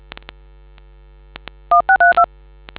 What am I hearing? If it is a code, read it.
DTMF, heard as telephone touch tones, tones of about 110 ms. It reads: 1632